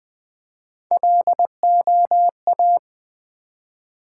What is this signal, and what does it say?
Morse code, heard as a keyed tone, about 20 wpm, 700 Hz. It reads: LOA
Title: Build A Word - 1 Letter At A Time - 4-Letter Words - 20wpm